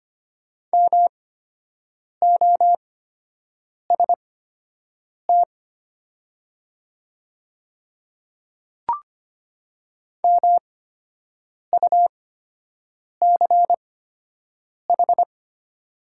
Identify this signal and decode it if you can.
Morse code, heard as a keyed tone, about 25 wpm, spaced as for 7 wpm, 700 Hz. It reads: MOST MUCH
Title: 8x Character Spacing - Top 200 Words - 25wpm